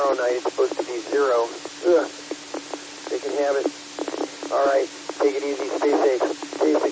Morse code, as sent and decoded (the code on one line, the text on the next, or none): none